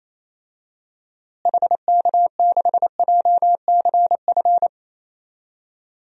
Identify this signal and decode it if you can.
Morse code, heard as a keyed tone, about 28 wpm, 700 Hz. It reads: HK6JCF
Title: DXCC-100 Call Signs - 6 Characters - Repeated 2x - 28wpm